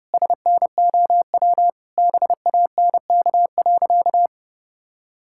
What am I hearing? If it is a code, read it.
Morse code, heard as a keyed tone, about 30 wpm, 700 Hz. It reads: SNOW BANK.
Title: Sentences from Top 1600 Words 30wpm